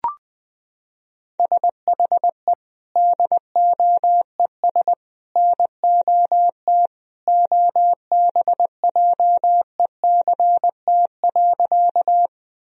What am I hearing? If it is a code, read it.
Morse code, heard as a keyed tone, about 19 wpm, 700 Hz. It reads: SHE DOES NOT OBJECT.
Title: Sentences from Top 1000 Words - Review Part 3 - 20wpm